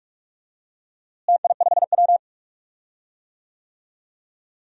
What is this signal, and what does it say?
Morse code, heard as a keyed tone, about 43 wpm, 700 Hz. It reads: TI5W
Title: DXCC-100 Call Signs - 4 Characters - Repeated 2x - 45wpm